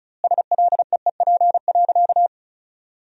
Morse code, heard as a keyed tone, about 36 wpm, 700 Hz. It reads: SLEEP.
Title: 2x Word-Spacing - Sentences from Top 500 Words - 35wpm